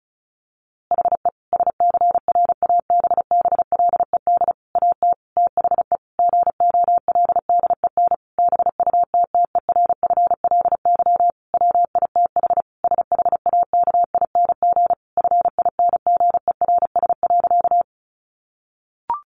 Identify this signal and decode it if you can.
Morse code, heard as a keyed tone, about 34 wpm, 700 Hz. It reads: HE SCRABBLED AT THE GOLDEN BUTTERFLY WITH SHAKING FINGERS.
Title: A Sound Of Thunder 35wpm